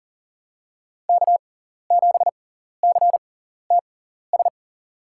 Morse code, heard as a keyed tone, about 40 wpm, spaced as for 14 wpm, 700 Hz. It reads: K7CTS
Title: USA Call Signs - 6x Character Spacing - 40wpm